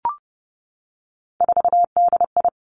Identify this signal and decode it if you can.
Morse code, heard as a keyed tone, about 30 wpm, 700 Hz. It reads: 4DI
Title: Three Challenging Letter-Number Combinations 30wpm